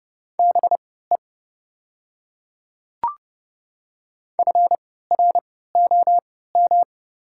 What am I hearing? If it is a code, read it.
Morse code, heard as a keyed tone, about 30 wpm, spaced as for 17 wpm, 700 Hz. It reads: BE FROM
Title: 3x Character Spacing - Top 300 Words - 30wpm